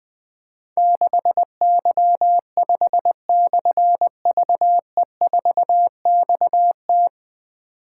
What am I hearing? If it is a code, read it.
Morse code, heard as a keyed tone, about 20 wpm, 700 Hz. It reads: 6Y5/VE4XT
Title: Difficult Call Signs 20wpm